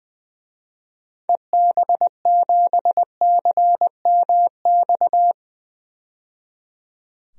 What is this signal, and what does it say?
Morse code, heard as a keyed tone, about 20 wpm, 700 Hz. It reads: EB7CMX
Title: DXCC-100 Call Signs - 6 Characters - 20wpm